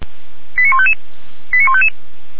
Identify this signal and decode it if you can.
Five-tone selective call (ZVEI), heard as quick runs of five tones, tones of about 70 ms. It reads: 89155 89155